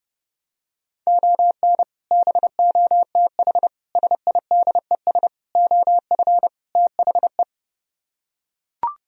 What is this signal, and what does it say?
Morse code, heard as a keyed tone, about 29 wpm, 700 Hz. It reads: ON BOTH SIDES OF THE